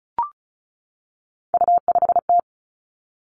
Morse code, heard as a keyed tone, about 36 wpm, 700 Hz. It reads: U5T
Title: Three Challenging Letter-Number Combinations 35wpm